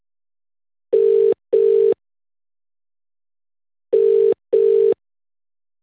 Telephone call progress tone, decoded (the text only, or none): ringback tone